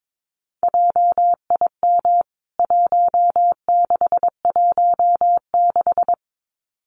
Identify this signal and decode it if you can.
Morse code, heard as a keyed tone, about 22 wpm, 700 Hz. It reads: JIM 1616
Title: CWT Exchange 22wpm